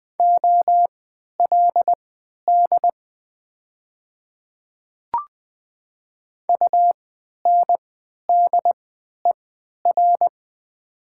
Morse code, heard as a keyed tone, about 20 wpm, spaced as for 11 wpm, 700 Hz. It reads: OLD UNDER
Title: 3x Character Spacing - Top 300 Words - 20wpm